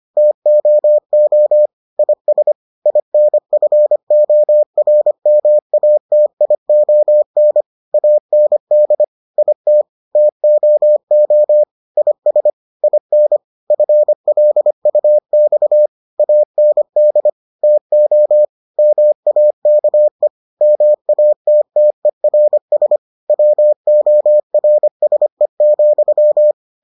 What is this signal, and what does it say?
Morse code, heard as a keyed tone, about 25 wpm, 600 Hz. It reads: TOO IS INFORMATION AND IT TOO IS IN FLUX AND TO MAKE MATTERS WORSE,